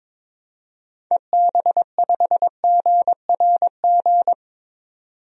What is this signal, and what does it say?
Morse code, heard as a keyed tone, about 22 wpm, 700 Hz. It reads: EB5GRG
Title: DXCC-100 Call Signs - 6 Characters - 22wpm